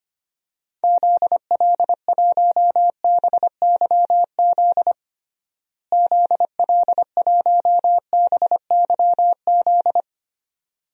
Morse code, heard as a keyed tone, about 25 wpm, 700 Hz. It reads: ZL1BYZ ZL1BYZ
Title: DXCC-100 Call Signs - 6 Characters - Repeated 2x - 25wpm